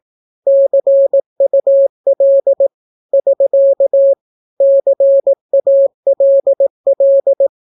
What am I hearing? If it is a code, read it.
Morse code, heard as a keyed tone, about 18 wpm, 550 Hz. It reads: CUL <SK> CALL
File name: naukapodstaw / skroty / skroty